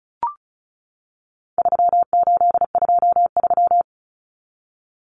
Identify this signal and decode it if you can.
Morse code, heard as a keyed tone, about 34 wpm, 700 Hz. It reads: 3823
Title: Common Cut Numbers - 4-Digit 35wpm